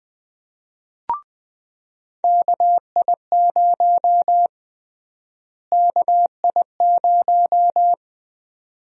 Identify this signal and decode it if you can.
Morse code, heard as a keyed tone, about 20 wpm, 700 Hz. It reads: KI0 KI0
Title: USA Call Sign Prefixes - Repeated 2x - 20wpm